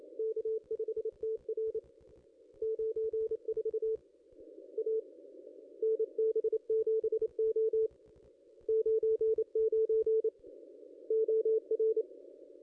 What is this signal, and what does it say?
Morse code, heard as a keyed tone, about 28 wpm, 440 Hz. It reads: K5TR 94 A NB7O 99 OR
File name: contests / ss-cw-k5tr-op-n5rz -2200